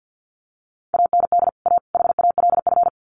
Morse code, heard as a keyed tone, about 50 wpm, 700 Hz. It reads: AND A HALF
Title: Sets of 3 Words 50wpm